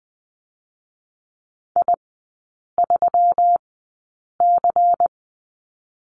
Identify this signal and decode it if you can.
Morse code, heard as a keyed tone, about 20 wpm, 700 Hz. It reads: I 3 C
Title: 172 Lesson 28e - Number 9 - Three Characters 20wpm